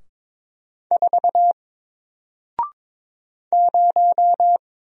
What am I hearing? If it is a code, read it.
Morse code, heard as a keyed tone, about 22 wpm, 700 Hz. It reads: 4 0
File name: Single-Letter-Number-22wpm